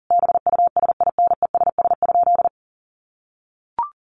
Morse code, heard as a keyed tone, about 40 wpm, 700 Hz. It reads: BUSINESS?
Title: Sentences from Top 1200 Words 40wpm